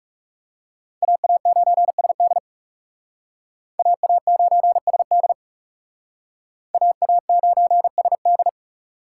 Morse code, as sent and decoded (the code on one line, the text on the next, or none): .- .- ----. ... -.. / .- .- ----. ... -.. / .- .- ----. ... -..
AA9SD AA9SD AA9SD